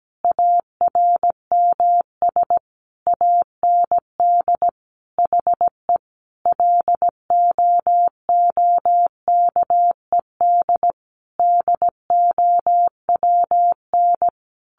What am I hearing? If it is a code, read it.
Morse code, heard as a keyed tone, about 17 wpm, 700 Hz. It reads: ARMS AND HE LOOKED DOWN
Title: A Sound Of Thunder 17wpm